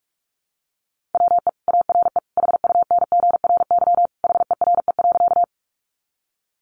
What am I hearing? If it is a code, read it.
Morse code, heard as a keyed tone, about 44 wpm, 700 Hz. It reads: WE ARE HUNGRY HERE.
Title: Sentences from Top 1400 Words 45wpm